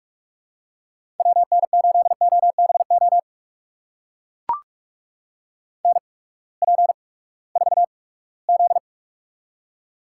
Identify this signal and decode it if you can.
Morse code, heard as a keyed tone, about 44 wpm, spaced as for 22 wpm, 700 Hz. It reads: WN8OBO NP4Z
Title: USA Call Signs - 8x Character Spacing - 45wpm